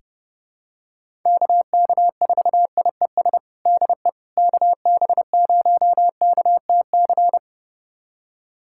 Morse code, heard as a keyed tone, about 30 wpm, 700 Hz. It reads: KK4IES DE KB0KTC
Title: USA Callsign DE Callsign 30wpm